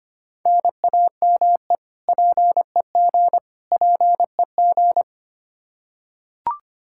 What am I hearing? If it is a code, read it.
Morse code, heard as a keyed tone, about 25 wpm, 700 Hz. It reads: NAME PEG PEG
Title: Practice QSO - Exchange 1 - Beginning - 25wpm